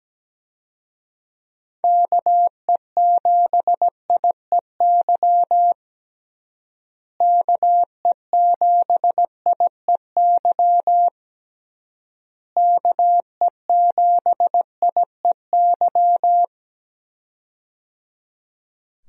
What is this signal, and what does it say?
Morse code, heard as a keyed tone, about 17 wpm, 700 Hz. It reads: KE7IEY KE7IEY KE7IEY
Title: USA Call Signs - 6 Characters - Repeated 3x - 17wpm